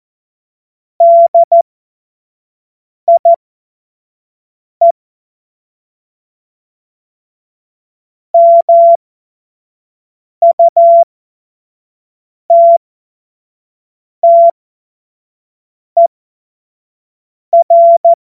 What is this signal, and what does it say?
Morse code, heard as a keyed tone, about 14 wpm, spaced as for 5 wpm, 680 Hz. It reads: DIE MUTTER
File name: Morsekurs / Witze / Witz 28. Dezember 2021, 25 F70